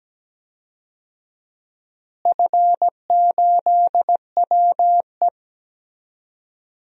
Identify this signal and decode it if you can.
Morse code, heard as a keyed tone, about 17 wpm, 700 Hz. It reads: F8WE
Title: DXCC-100 Call Signs - 4 Characters - Repeated 2x - 17wpm